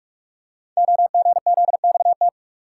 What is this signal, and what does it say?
Morse code, heard as a keyed tone, about 44 wpm, 700 Hz. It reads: OGZXT
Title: Code Groups - 5 Letters - 45wpm